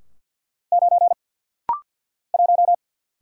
Morse code, heard as a keyed tone, about 49 wpm, 700 Hz. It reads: 9 1
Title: Single Letter-Number - Rapid-Fire 50wpm